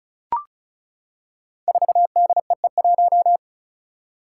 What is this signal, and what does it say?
Morse code, heard as a keyed tone, about 36 wpm, 700 Hz. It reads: 4DEE1
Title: Five Challenging Letter-Number Combinations 35wpm